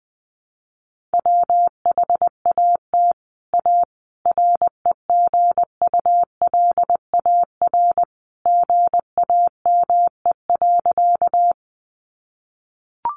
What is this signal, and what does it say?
Morse code, heard as a keyed tone, about 20 wpm, 700 Hz. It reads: WHAT A REGULAR GAME.
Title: Sentences from Top 900 Words 20wpm